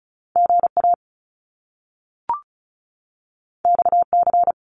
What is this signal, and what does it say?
Morse code, heard as a keyed tone, about 36 wpm, 700 Hz. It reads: GA XC
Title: Top Two Letter Combinations in Words 35wpm